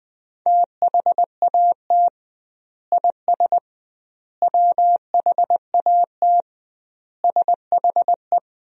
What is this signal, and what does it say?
Morse code, heard as a keyed tone, about 20 wpm, 700 Hz. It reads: THAT IS WHAT SHE
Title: 2x Word-Spacing - Sentences from Top 100 Words - 20wpm